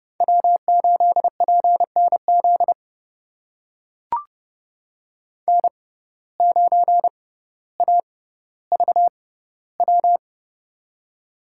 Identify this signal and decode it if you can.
Morse code, heard as a keyed tone, about 30 wpm, spaced as for 15 wpm, 700 Hz. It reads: W8PNZ N9AVW
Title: USA Call Signs - 5 Characters - 6x Character Spacing - 30wpm